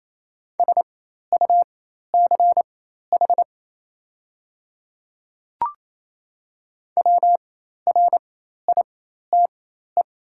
Morse code, heard as a keyed tone, about 28 wpm, spaced as for 13 wpm, 700 Hz. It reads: SUCH WRITE
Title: 4x Character Spacing - Top 200 Words - 28wpm